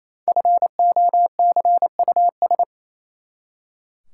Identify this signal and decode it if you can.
Morse code, heard as a keyed tone, about 28 wpm, 700 Hz. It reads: FOCUS
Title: Five Letter Words - 28wpm